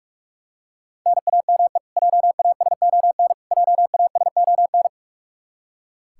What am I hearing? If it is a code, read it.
Morse code, heard as a keyed tone, about 43 wpm, 700 Hz. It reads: NAME JASON JASON
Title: Practice QSO - Exchange 1 - Beginning - 45wpm